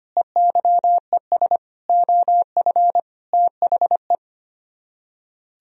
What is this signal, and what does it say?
Morse code, heard as a keyed tone, about 32 wpm, 700 Hz. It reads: EYES OF THE